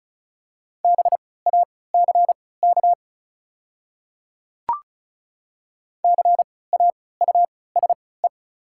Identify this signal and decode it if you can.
Morse code, heard as a keyed tone, about 36 wpm, spaced as for 20 wpm, 700 Hz. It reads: BACK CAUSE
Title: 3x Character Spacing - Top 200 Words - 35wpm